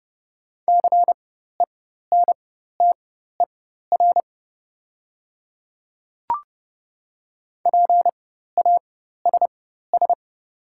Morse code, heard as a keyed tone, about 30 wpm, spaced as for 14 wpm, 690 Hz. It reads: CENTER PASS
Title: 4x Character Spacing - Top 500 Words - 30wpm